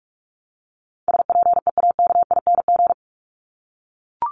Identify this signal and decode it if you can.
Morse code, heard as a keyed tone, about 44 wpm, 700 Hz. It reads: SPEAKING
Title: ING Suffix - Top 100 Words with Base Words 45wpm